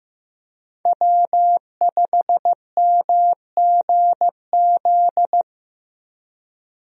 Morse code, heard as a keyed tone, about 15 wpm, 700 Hz. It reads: W5MGZ
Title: USA Call Signs - 5 Characters - Repeated 2x - 15wpm